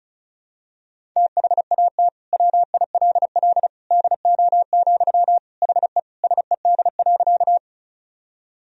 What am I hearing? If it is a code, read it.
Morse code, heard as a keyed tone, about 36 wpm, 700 Hz. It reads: THAT WILL DO, HE SED.
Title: Sentences from Top 100 Words and Abbreviations 35wpm